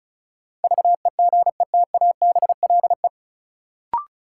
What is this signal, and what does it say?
Morse code, heard as a keyed tone, about 36 wpm, 700 Hz. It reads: VEGETABLE
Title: Sentences from Top 1300 Words - New Word Drill - 35wpm